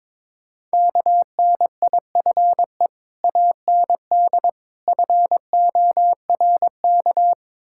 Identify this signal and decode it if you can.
Morse code, heard as a keyed tone, about 22 wpm, 690 Hz. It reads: KNIFE AND FORK
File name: Easy Words - Speed-Racing with Hints - 22wpm